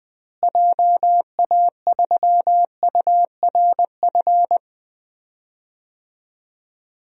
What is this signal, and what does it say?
Morse code, heard as a keyed tone, about 20 wpm, 700 Hz. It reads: JA3URF
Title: DXCC-100 Call Signs - 6 Characters - Repeated 2x - 20wpm